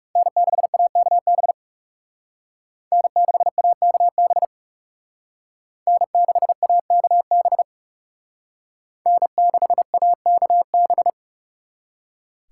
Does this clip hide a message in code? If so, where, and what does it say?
Yes: Morse code from 0.2 to 11.1 s, heard as a keyed tone, about 39 wpm, 700 Hz. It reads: N6AKB N6AKB N6AKB N6AKB